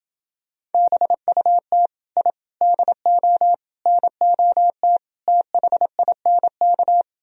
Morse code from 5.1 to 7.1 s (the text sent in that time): THINK